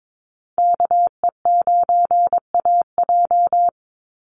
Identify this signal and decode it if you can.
Morse code, heard as a keyed tone, about 22 wpm, 700 Hz. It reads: KE9AJ
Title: NA SOTA Chaser Callsigns 22wpm